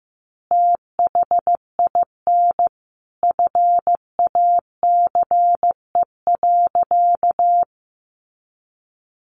Morse code, heard as a keyed tone, about 15 wpm, 700 Hz. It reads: THIN FACE.
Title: A Sound Of Thunder 15wpm